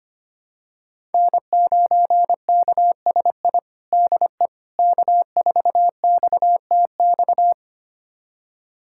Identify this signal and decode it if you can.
Morse code, heard as a keyed tone, about 25 wpm, 700 Hz. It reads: N9KSI DE K4XTX
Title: USA Callsign DE Callsign - 5 Characters - 25wpm